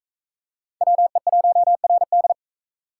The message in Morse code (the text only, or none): WE1RD